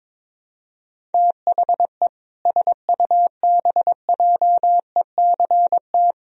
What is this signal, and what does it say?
Morse code, heard as a keyed tone, about 22 wpm, 700 Hz. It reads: THE SUBJECT
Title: Sentences from Top 1000 Words - Review Part 4 - 22wpm